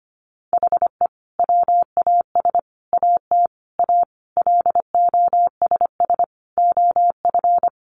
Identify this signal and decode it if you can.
Morse code, heard as a keyed tone, about 25 wpm, 700 Hz. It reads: HE WAS AT A LOSS OF
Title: Sentences from Top 1400 Words 25wpm